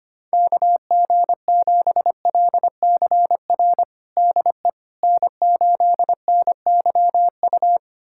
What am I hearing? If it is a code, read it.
Morse code, heard as a keyed tone, about 25 wpm, 700 Hz. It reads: KG7LCR DE N8NYU
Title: USA Callsign DE Callsign 25wpm